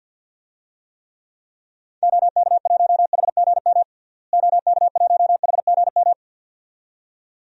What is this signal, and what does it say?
Morse code, heard as a keyed tone, about 53 wpm, 700 Hz. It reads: OK1HGM OK1HGM
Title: DXCC-100 Call Signs - 6 Characters - Repeated 2x - 50wpm